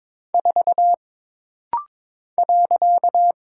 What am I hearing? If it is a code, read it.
Morse code, heard as a keyed tone, about 22 wpm, 700 Hz. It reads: 4 .
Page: This rapid-fire timing is geared to push your ICR (Instant Character Recognition) skills.